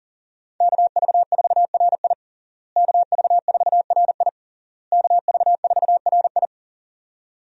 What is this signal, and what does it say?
Morse code, heard as a keyed tone, about 38 wpm, 700 Hz. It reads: KV4RI KV4RI KV4RI
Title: USA Call Signs - 5 Characters - Repeated 3x - 40wpm